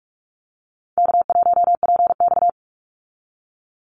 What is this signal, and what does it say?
Morse code, heard as a keyed tone, about 44 wpm, 700 Hz. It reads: K1PX
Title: Easy Call Signs - 45wpm